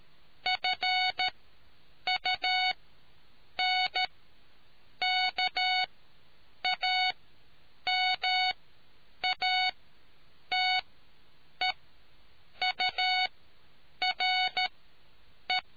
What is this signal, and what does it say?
Morse code, heard as a keyed tone, about 13 wpm, spaced as for 7 wpm, 740 Hz. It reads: FUNKAMATEURE